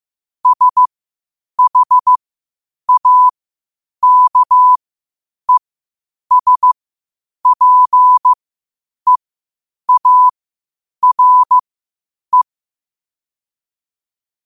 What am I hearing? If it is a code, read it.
Morse code, heard as a keyed tone, about 15 wpm, spaced as for 8 wpm, 1000 Hz. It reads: SHAKESPEARE
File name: qotdfiles / 8wpm